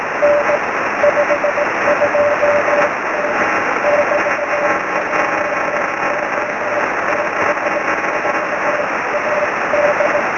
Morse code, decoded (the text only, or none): none